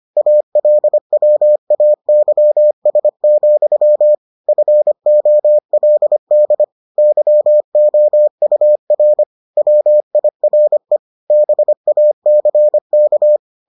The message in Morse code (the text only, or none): ALWAYS, FOLD YOUR WIRE BACK